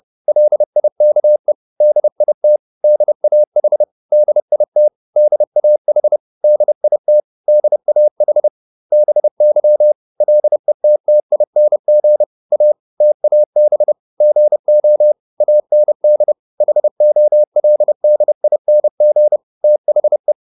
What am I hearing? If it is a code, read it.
Morse code, heard as a keyed tone, about 30 wpm, 600 Hz. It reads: LIKE DIT DAH DIT DAH DIT DAH BY LETTING A TAB GO AND HOLDING THE